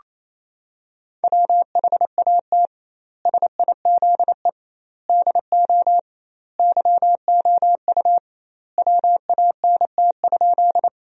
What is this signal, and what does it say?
Morse code, heard as a keyed tone, about 28 wpm, 700 Hz. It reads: WHAT SIZE DO YOU WANT?
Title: 2x Word-Spacing - Sentences from Top 400 Words - 28wpm